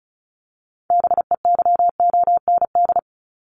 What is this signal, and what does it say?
Morse code, heard as a keyed tone, about 34 wpm, 700 Hz. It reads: BEYOND
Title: Top 751-1000 Words 35wpm